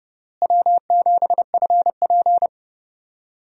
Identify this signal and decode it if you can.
Morse code, heard as a keyed tone, about 30 wpm, 700 Hz. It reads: W7FP
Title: USA Call Signs - 4 Characters - Repeated 3x - 30wpm